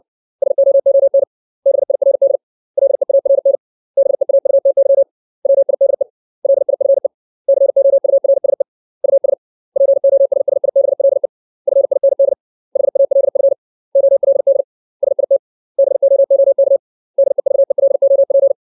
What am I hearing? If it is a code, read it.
Morse code, heard as a keyed tone, about 60 wpm, 550 Hz. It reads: SOON BIRD LEARN BEAUTY WIDE LIFE COURSE US POSSIBLE FIND HALF OLD SIT BOOK DEVELOP
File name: 632 words / 632 Words 60WPM